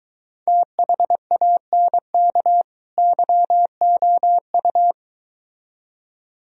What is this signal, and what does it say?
Morse code, heard as a keyed tone, about 23 wpm, 700 Hz. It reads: THANK YOU